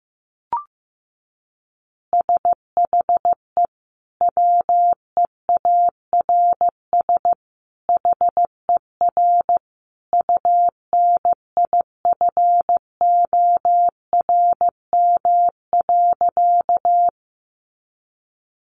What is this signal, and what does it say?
Morse code, heard as a keyed tone, about 15 wpm, 700 Hz. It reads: SHE WEARS HER UNIFORM.
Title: Easy Phrases 15wpm